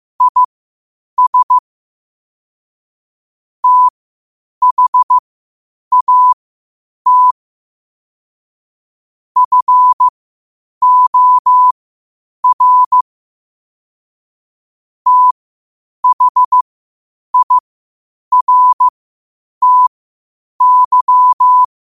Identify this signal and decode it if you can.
Morse code, heard as a keyed tone, about 15 wpm, spaced as for 8 wpm, 1000 Hz. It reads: IS THAT FOR THIRTY
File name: qotdfiles / 8wpm